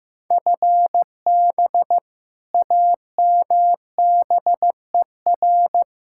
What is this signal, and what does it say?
Morse code, heard as a keyed tone, about 15 wpm, 700 Hz. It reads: FB AMBER
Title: Practice QSO - Exchange 3 - More Info - 15wpm